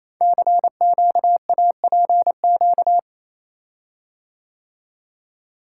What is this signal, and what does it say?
Morse code, heard as a keyed tone, about 28 wpm, 700 Hz. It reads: CQAPQ
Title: Code Groups - 5 Letters - 28wpm